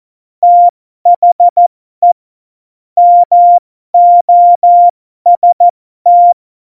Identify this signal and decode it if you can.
Morse code, heard as a keyed tone, about 14 wpm, spaced as for 12 wpm, 700 Hz. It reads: THE MOST
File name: Morsekurs / Witze